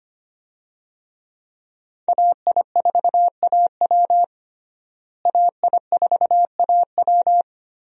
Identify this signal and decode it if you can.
Morse code, heard as a keyed tone, about 25 wpm, 700 Hz. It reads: AI4AW AI4AW
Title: USA Call Signs - 5 Characters - Repeated 3x - 25wpm